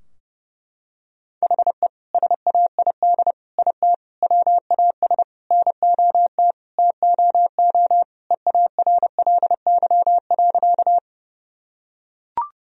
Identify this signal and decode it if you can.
Morse code, heard as a keyed tone, about 30 wpm, 700 Hz. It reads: HE SAID IT WAS NOT TOO EARLY.
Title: Sentences from Top 1000 Words - Review Part 2 - 30wpm